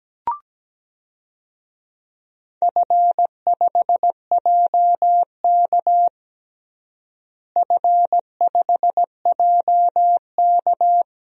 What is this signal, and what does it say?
Morse code, heard as a keyed tone, about 17 wpm, 700 Hz. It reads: F5JK F5JK
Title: DXCC-100 Call Signs - 4 Characters - Repeated 3x - 17wpm